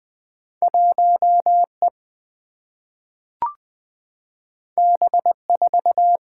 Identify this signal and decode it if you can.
Morse code, heard as a keyed tone, about 20 wpm, 700 Hz. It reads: 1E B4
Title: Two Letter-Number Combinations - 20wpm